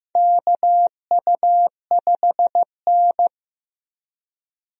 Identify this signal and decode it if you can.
Morse code, heard as a keyed tone, about 15 wpm, 700 Hz. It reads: KU5N
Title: USA Call Signs - 4 Characters - 15wpm